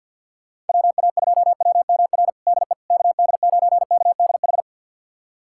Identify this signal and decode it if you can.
Morse code, heard as a keyed tone, about 53 wpm, 690 Hz. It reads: WA2WMR DE KD9KDH
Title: USA Callsign DE Callsign - 6 Characters - 50wpm